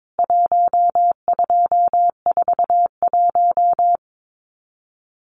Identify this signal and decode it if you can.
Morse code, heard as a keyed tone, about 22 wpm, 700 Hz. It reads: 1241
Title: CWT Exchange 22wpm